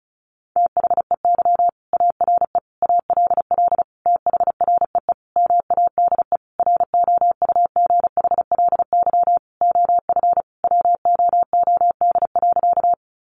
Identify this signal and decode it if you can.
Morse code, heard as a keyed tone, about 34 wpm, 700 Hz. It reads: THEY ARE ALL THREE MADE ROUGHLY OF WOOD.
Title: Sentences from Top 1500 Words 35wpm